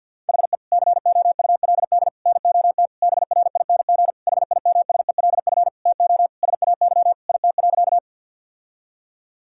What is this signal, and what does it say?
Morse code, heard as a keyed tone, about 52 wpm, 700 Hz. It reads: HE COULD NOT BRING HIMSELF TO SAY IT.
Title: Sentences from Top 1600 Words 50wpm